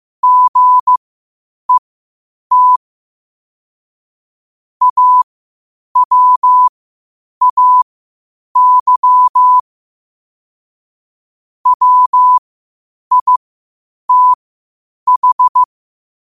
Morse code, heard as a keyed tone, about 15 wpm, spaced as for 8 wpm, 1000 Hz. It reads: GET AWAY WITH
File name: qotdfiles / 8wpm